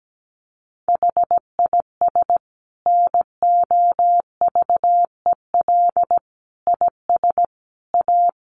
Morse code, heard as a keyed tone, about 17 wpm, 700 Hz. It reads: HIS NOVEL IS A
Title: Sentences from Top 1300 Words 17wpm